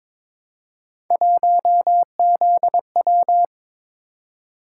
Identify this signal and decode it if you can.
Morse code, heard as a keyed tone, about 22 wpm, 700 Hz. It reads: 1ZW
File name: practice / 22wpm / Three Letter-Number Combinations - 22wpm